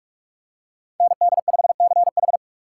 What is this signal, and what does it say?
Morse code, heard as a keyed tone, about 44 wpm, 700 Hz. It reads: ND5CH